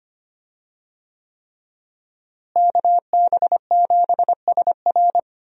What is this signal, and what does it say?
Morse code, heard as a keyed tone, about 25 wpm, 700 Hz. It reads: KB7SR
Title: USA Call Signs - 5 Characters - Repeated 3x - 25wpm